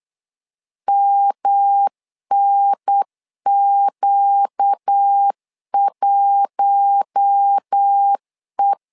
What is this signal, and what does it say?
Morse code, heard as a keyed tone, about 8 wpm, 800 Hz. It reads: MNQ1E
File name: LEZIONI / 4 Q1 8wpm 3035